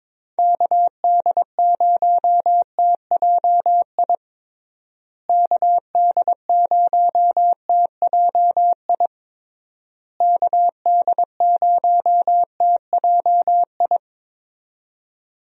Morse code, heard as a keyed tone, about 22 wpm, 700 Hz. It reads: KD0TJI KD0TJI KD0TJI
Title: USA Call Signs - 6 Characters - Repeated 3x - 22wpm